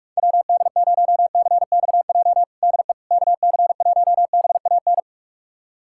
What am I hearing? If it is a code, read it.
Morse code, heard as a keyed tone, about 41 wpm, 690 Hz. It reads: WD0CXJ DE KC1BAN
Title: USA Callsign DE Callsign - 6 Characters - 45wpm